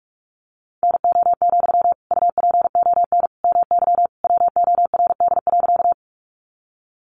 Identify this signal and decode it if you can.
Morse code, heard as a keyed tone, about 44 wpm, 700 Hz. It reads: NO, UPON MY WORD.
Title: Sentences from Top 1100 Words 45wpm